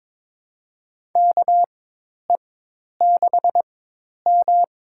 Morse code, heard as a keyed tone, about 21 wpm, spaced as for 10 wpm, 700 Hz. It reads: KE6M
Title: USA Call Signs - 4x Character Spacing - 22wpm